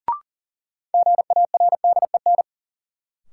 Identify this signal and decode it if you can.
Morse code, heard as a keyed tone, about 37 wpm, 700 Hz. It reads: GARDEN
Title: Sentences from Top 700 Words - New Word Drill - 40wpm